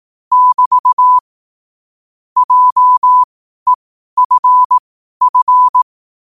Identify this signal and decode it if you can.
Morse code, heard as a keyed tone, about 18 wpm, spaced as for 12 wpm, 1000 Hz. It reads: <BT> JEFF
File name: qotdfiles / 12wpm